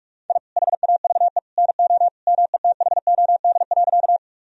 Morse code, heard as a keyed tone, about 45 wpm, 700 Hz. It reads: I HAVE NO METHOD.
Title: Sentences from Top 1000 Words - Review Part 3 - 45wpm